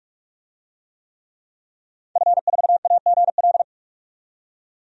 Morse code, heard as a keyed tone, about 42 wpm, 700 Hz. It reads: F4AGL
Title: DXCC-100 Call Signs - 5 Characters - 45wpm